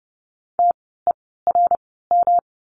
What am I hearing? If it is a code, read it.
Morse code, heard as a keyed tone, about 29 wpm, spaced as for 17 wpm, 700 Hz. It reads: TERM
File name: course / 30:10wpm / 108 Lesson 18f - Letter m - Words 30wpm